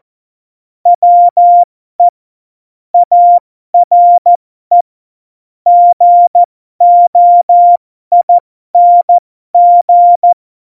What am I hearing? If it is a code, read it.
Morse code, heard as a keyed tone, about 14 wpm, spaced as for 12 wpm, 700 Hz. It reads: WE ARE GOING